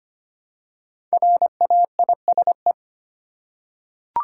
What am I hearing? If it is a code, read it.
Morse code, heard as a keyed tone, about 25 wpm, 700 Hz. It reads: RAISE
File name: 5 Letter Words - First 3 Letters Given - 25wpm